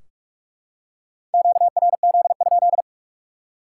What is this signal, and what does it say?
Morse code, heard as a keyed tone, about 44 wpm, 700 Hz. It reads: QRZ?
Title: Top 180 QSO Elements 45wpm